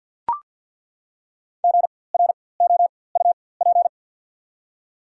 Morse code, heard as a keyed tone, about 48 wpm, spaced as for 23 wpm, 690 Hz. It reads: GROUP
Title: 4x Character Spacing - Top 300 Words - 50wpm